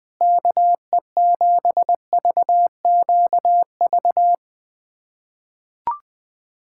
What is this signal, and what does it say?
Morse code, heard as a keyed tone, about 20 wpm, 700 Hz. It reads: KE7VQV
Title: USA Callsign DE Callsign - 6 Characters - 20wpm